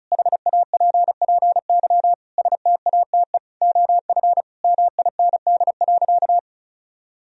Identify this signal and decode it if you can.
Morse code, heard as a keyed tone, about 34 wpm, 700 Hz. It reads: HAPPY STATE OF MIND.
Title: Sentences from Top 1000 Words - Review Part 2 - 35wpm